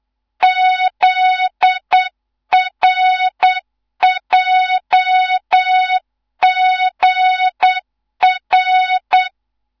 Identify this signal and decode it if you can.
Morse code, heard as a keyed tone, about 8 wpm, 3700 Hz, with noise. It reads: ZRJGR
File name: Morze-HA2MN / hang / 15 GYAKORLAT